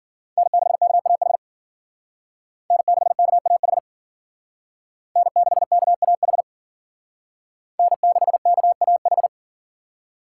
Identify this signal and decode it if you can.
Morse code, heard as a keyed tone, about 51 wpm, 700 Hz. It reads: N6KAH N6KAH N6KAH N6KAH